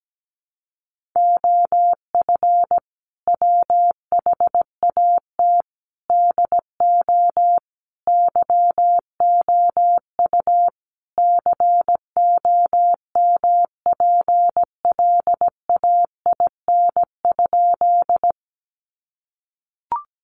Morse code, heard as a keyed tone, about 17 wpm, 700 Hz. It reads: OF WHAT DO YOU COMPLAIN?
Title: Sentences from Top 1300 Words 17wpm